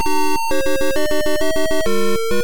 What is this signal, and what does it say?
There is no code.